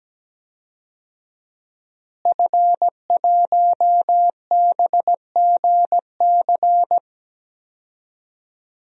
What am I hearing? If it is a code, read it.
Morse code, heard as a keyed tone, about 17 wpm, 700 Hz. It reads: F1BGC
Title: DXCC-100 Call Signs - 5 Characters - Repeated 3x - 17wpm